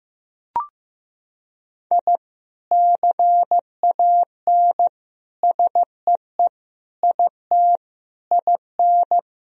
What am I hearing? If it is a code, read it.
Morse code, heard as a keyed tone, about 15 wpm, 700 Hz. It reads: I CAN SEE IT IN
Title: Sentences from Top 1000 Words - Review Part 1 - 15wpm